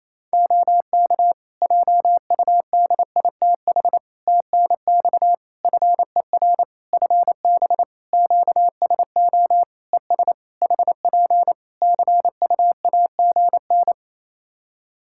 Morse code, heard as a keyed tone, about 28 wpm, 700 Hz. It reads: OK JUDITH TNX FER FB QSO ES HP CUAGN
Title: Practice QSO - Exchange 4 - Ending - 28wpm